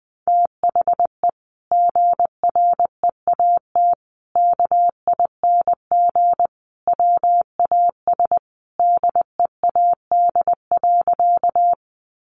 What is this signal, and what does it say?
Morse code, heard as a keyed tone, about 20 wpm, 700 Hz. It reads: THE GREAT KING WAS DEAD.
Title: Sentences from Top 1500 Words 20wpm